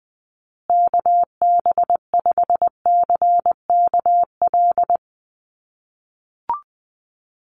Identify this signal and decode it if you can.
Morse code, heard as a keyed tone, about 20 wpm, 700 Hz. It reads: KB5CKL
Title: USA Call Signs Repeated 3x 20wpm